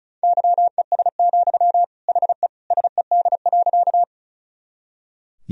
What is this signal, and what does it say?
Morse code, heard as a keyed tone, about 36 wpm, 700 Hz. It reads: YES, HE SED.
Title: Sentences from Top 500 Words and Abbreviations 35wpm